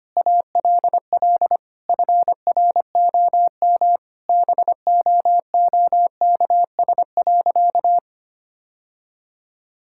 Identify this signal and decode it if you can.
Morse code, heard as a keyed tone, about 25 wpm, 700 Hz. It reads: ALL FROM BOOKS.
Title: Sentences from Top 1000 Words - Review Part 6 - 25wpm